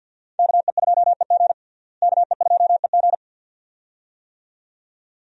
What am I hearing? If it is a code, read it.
Morse code, heard as a keyed tone, about 52 wpm, 700 Hz. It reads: KE2EG KE2EG
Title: USA Call Signs - 5 Characters - Repeated 2x - 50wpm